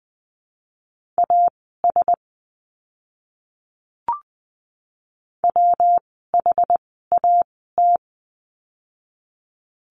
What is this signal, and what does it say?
Morse code, heard as a keyed tone, about 20 wpm, spaced as for 14 wpm, 700 Hz. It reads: AS WHAT
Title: 244 Lesson 39f - Letter z - Words 20wpm